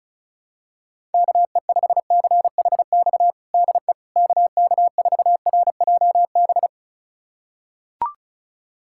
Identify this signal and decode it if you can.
Morse code, heard as a keyed tone, about 34 wpm, 700 Hz. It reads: KE5CHX DE KK4RJB
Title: USA Callsign DE Callsign - 6 Characters - 35wpm